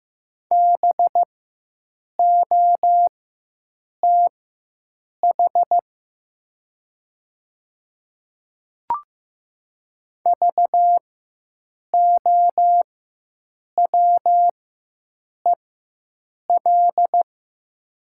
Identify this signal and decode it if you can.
Morse code, heard as a keyed tone, about 15 wpm, spaced as for 7 wpm, 700 Hz. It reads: BOTH VOWEL
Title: 4x Character Spacing - Top 500 Words - 15wpm